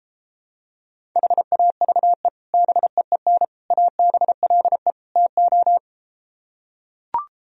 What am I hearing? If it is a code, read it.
Morse code, heard as a keyed tone, about 33 wpm, 700 Hz. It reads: HAVE BEEN ABLE TO